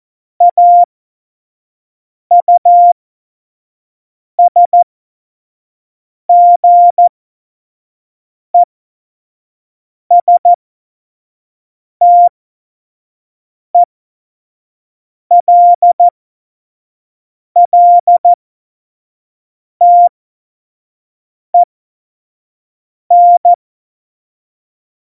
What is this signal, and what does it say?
Morse code, heard as a keyed tone, about 14 wpm, spaced as for 5 wpm, 700 Hz. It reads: AUSGESTELLTEN
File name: Morsekurs / Witze / Witz 08. Mai 2018, 25 F70